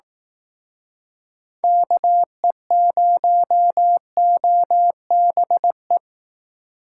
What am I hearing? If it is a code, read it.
Morse code, heard as a keyed tone, about 18 wpm, 700 Hz. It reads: KE0OBE
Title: Speed Racing - USA Call Signs 15wpm